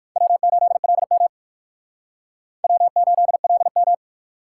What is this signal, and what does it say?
Morse code, heard as a keyed tone, about 56 wpm, 700 Hz. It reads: W8LM W8LM
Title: Speed Racing - USA 1x2 Call Signs 45wpm